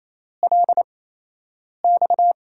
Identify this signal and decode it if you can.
Morse code, heard as a keyed tone, about 28 wpm, spaced as for 8 wpm, 700 Hz. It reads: LX
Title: USA Call Signs - 5 Characters - 8x Character Spacing - 28wpm